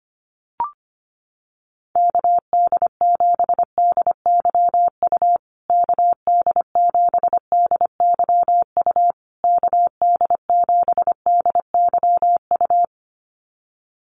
Morse code, heard as a keyed tone, about 25 wpm, 700 Hz. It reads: KD7DYU KD7DYU KD7DYU
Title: USA Call Signs Repeated 3x 25wpm